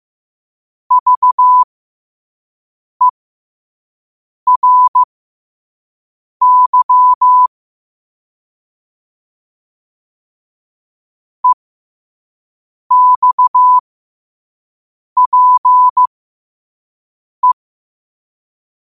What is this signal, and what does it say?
Morse code, heard as a keyed tone, about 15 wpm, spaced as for 5 wpm, 1000 Hz. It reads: VERY EXPE
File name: qotdfiles / 5wpm